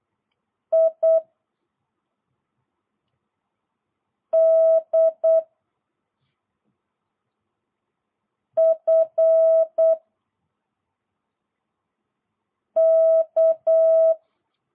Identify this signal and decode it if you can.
Morse code, heard as a keyed tone, about 8 wpm, 660 Hz. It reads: I D F K